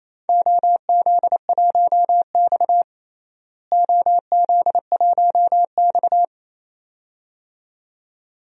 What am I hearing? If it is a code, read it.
Morse code, heard as a keyed tone, about 28 wpm, 700 Hz. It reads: OZ1X OZ1X
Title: DXCC-100 Call Signs - 4 Characters - Repeated 2x - 28wpm